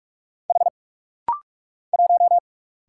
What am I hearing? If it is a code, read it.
Morse code, heard as a keyed tone, about 46 wpm, 700 Hz. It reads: H 1
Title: Single Letter-Number Alt - Rapid-Fire 45wpm